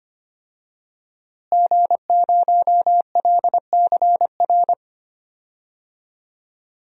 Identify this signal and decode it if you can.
Morse code, heard as a keyed tone, about 25 wpm, 700 Hz. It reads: G0LCR
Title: DXCC-100 Call Signs - 5 Characters - Repeated 2x - 25wpm